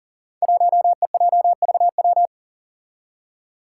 Morse code, heard as a keyed tone, about 43 wpm, 700 Hz. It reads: 1EJVW
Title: Five Challenging Letter-Number Combinations 40wpm